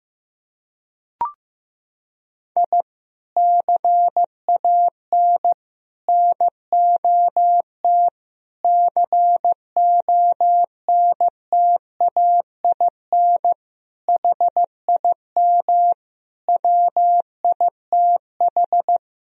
Morse code, heard as a keyed tone, about 14 wpm, 700 Hz. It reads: I CAN NOT CONTAIN HIM WITH
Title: Sentences from Top 1000 Words - Review Part 3 - 15wpm